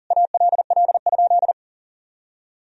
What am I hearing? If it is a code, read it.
Morse code, heard as a keyed tone, about 37 wpm, 700 Hz. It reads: ALL?
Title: Sentences from Top 1000 Words - Review Part 4 - 40wpm